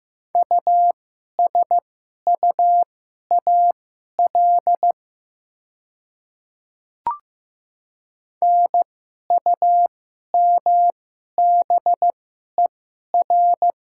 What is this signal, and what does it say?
Morse code, heard as a keyed tone, about 15 wpm, spaced as for 11 wpm, 700 Hz. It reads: USUAL NUMBER
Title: 2x Character Spacing - Top 500 Words - 15wpm